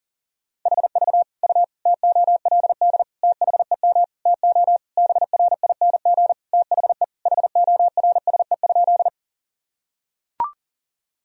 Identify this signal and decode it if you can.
Morse code, heard as a keyed tone, about 40 wpm, 700 Hz. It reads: HV U TOLD THEM TO BRING THE HORSE?
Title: Sentences from Top 500 Words and Abbreviations 40wpm